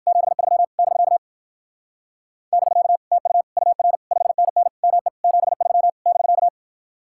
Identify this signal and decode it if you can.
Morse code, heard as a keyed tone, about 56 wpm, 700 Hz. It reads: 73 <BK> <BK> TU UR 5NN ME 73 <BK>